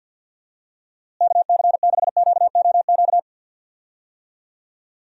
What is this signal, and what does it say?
Morse code, heard as a keyed tone, about 48 wpm, 700 Hz. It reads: KC6QOQ
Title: USA Call Signs - 6 Characters - Repeated 3x - 50wpm